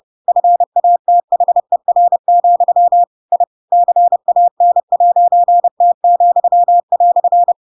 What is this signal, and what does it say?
Morse code, heard as a keyed tone, about 30 wpm, 700 Hz. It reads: FATHER, I CAN'T,"
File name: Pinocchio / Pinocchio ch7 30 30 ik1hge